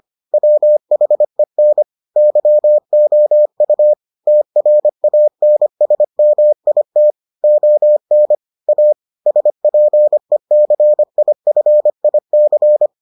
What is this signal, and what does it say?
Morse code, heard as a keyed tone, about 25 wpm, 600 Hz. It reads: WHEN YOU TRANSMIT ON A SPECIFIC